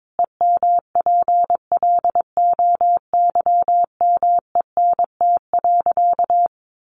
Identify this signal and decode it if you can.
Morse code, heard as a keyed tone, about 22 wpm, 700 Hz. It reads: EMPLOYMENT.
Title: Sentences from Top 1500 Words 22wpm